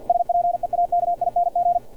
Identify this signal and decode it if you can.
Morse code, heard as a keyed tone, about 50 wpm, 700 Hz, with radio noise. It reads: SPEAKING
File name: archived / CRSnet